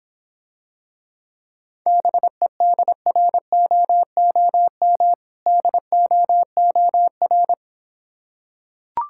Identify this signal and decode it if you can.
Morse code, heard as a keyed tone, about 26 wpm, 700 Hz. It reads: BEDROOM DOOR